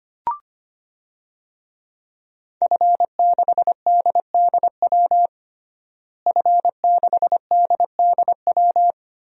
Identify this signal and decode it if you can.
Morse code, heard as a keyed tone, about 25 wpm, 700 Hz. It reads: F6DDW F6DDW
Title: DXCC-100 Call Signs - 5 Characters - Repeated 2x - 25wpm